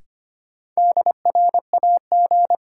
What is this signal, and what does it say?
Morse code, heard as a keyed tone, about 25 wpm, 700 Hz. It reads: DRAG
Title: Sentences from Top 1500 Words - New Word Drill Alt - 25wpm